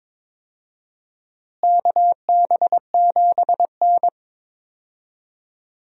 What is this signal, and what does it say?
Morse code, heard as a keyed tone, about 21 wpm, 700 Hz. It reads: KB7N
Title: USA Call Signs - 4 Characters - Repeated 3x - 22wpm